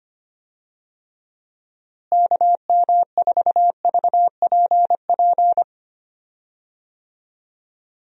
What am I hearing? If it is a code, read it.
Morse code, heard as a keyed tone, about 25 wpm, 700 Hz. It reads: KM4VPP
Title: USA Call Signs - 6 Characters - Repeated 3x - 25wpm